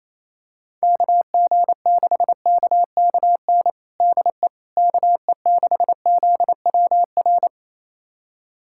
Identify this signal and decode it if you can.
Morse code, heard as a keyed tone, about 27 wpm, 700 Hz. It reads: KG6KKN DE KE6ZWR
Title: USA Callsign DE Callsign - 6 Characters - 28wpm